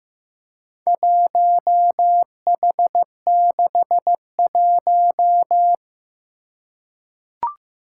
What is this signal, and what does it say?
Morse code, heard as a keyed tone, about 15 wpm, 700 Hz. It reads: 1H61
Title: Four Challenging Letter-Number Combinations 15wpm